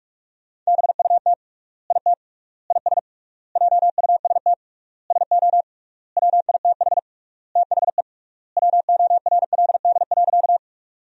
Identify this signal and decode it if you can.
Morse code, heard as a keyed tone, about 47 wpm, 700 Hz. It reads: BUT IT IS JUST SO WITH THE WORLD.
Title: 3x Word-Spacing - Sentences from Top 200 Words - 45wpm